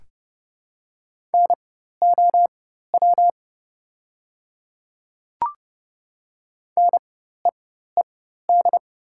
Morse code, heard as a keyed tone, about 30 wpm, spaced as for 14 wpm, 700 Hz. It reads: NOW NEED
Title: 4x Character Spacing - Top 200 Words - 30wpm